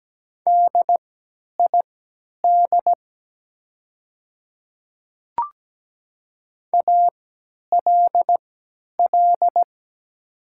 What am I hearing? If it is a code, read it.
Morse code, heard as a keyed tone, about 17 wpm, spaced as for 10 wpm, 690 Hz. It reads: DID ALL
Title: 3x Character Spacing - Top 100 Words - 17wpm